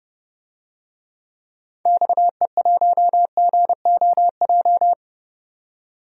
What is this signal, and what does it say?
Morse code, heard as a keyed tone, about 30 wpm, 700 Hz. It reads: XE1GOJ
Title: DXCC-100 Call Signs - 6 Characters - 30wpm